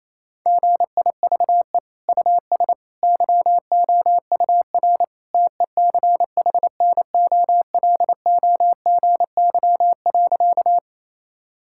Morse code, heard as a keyed tone, about 28 wpm, 700 Hz. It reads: GIVE US YOUR TECHNOLOGY.
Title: Sentences from Top 1000 Words - Review Part 5 - 28wpm